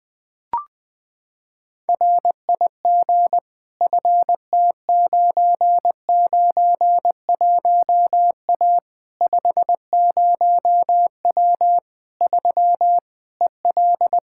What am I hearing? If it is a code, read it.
Morse code, heard as a keyed tone, about 21 wpm, 700 Hz. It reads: RIG FT991A 50W 3 EL
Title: Practice QSO - Exchange 2 - Info - 20wpm